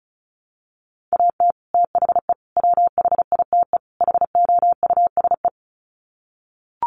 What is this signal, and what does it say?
Morse code, heard as a keyed tone, about 34 wpm, 700 Hz. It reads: AT THE WHITE HOUSE
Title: Sets of 4 Words 35wpm